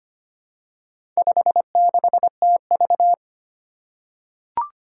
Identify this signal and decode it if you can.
Morse code, heard as a keyed tone, about 25 wpm, 700 Hz. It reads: 56TV
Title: Four Challenging Letter-Number Combinations 25wpm